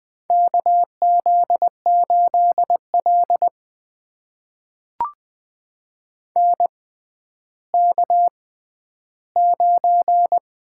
Morse code, heard as a keyed tone, about 20 wpm, spaced as for 11 wpm, 700 Hz. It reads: KZ8L NK9
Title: USA Call Signs - 4 Characters - 6x Character Spacing - 20wpm